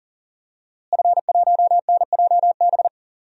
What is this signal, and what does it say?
Morse code, heard as a keyed tone, about 40 wpm, 700 Hz. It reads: F1NJB
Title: DXCC-100 Call Signs - 5 Characters - Repeated 2x - 40wpm